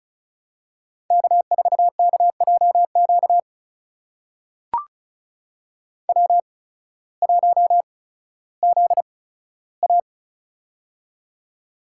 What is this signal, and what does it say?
Morse code, heard as a keyed tone, about 36 wpm, spaced as for 16 wpm, 700 Hz. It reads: K4KJQ W1ZA
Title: USA Call Signs - 8x Character Spacing - 35wpm